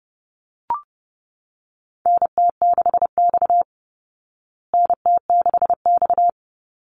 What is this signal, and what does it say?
Morse code, heard as a keyed tone, about 30 wpm, 700 Hz. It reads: NT6X NT6X
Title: Call Signs Repeated 3x - 30wpm